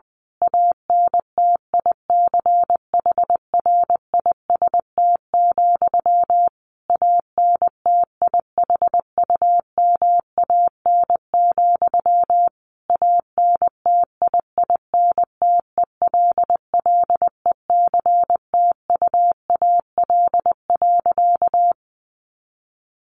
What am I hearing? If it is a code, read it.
Morse code, heard as a keyed tone, about 20 wpm, 700 Hz. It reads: ANTICHRIST, ANTIHUMAN, ANTIINTELLECTUAL.
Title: A Sound Of Thunder 20wpm